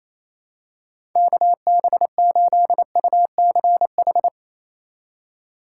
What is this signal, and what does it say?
Morse code, heard as a keyed tone, about 28 wpm, 700 Hz. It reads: KB8UCH
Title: USA Call Signs - 6 Characters - 4x Character Spacing - 28wpm